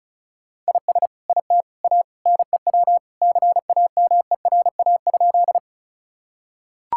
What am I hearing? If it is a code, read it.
Morse code, heard as a keyed tone, about 34 wpm, 700 Hz. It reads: IS IT A NEW CAMERA?
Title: Sentences from Top 1000 Words - Review Part 5 - 35wpm